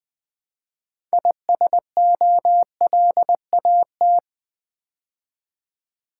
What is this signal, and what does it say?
Morse code, heard as a keyed tone, about 20 wpm, 700 Hz. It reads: ISOLAT
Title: Build A Word - 3 Letters At A Time - 9-Letter Words - 20wpm